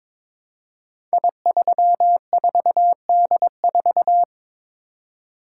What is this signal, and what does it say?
Morse code, heard as a keyed tone, about 22 wpm, 700 Hz. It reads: I34D4
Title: Code Groups - 5 Letter:Numbers - 22wpm